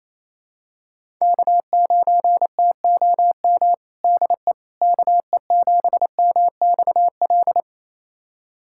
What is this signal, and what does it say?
Morse code, heard as a keyed tone, about 28 wpm, 700 Hz. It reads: K9TOM DE KE7MXL
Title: USA Callsign DE Callsign 28wpm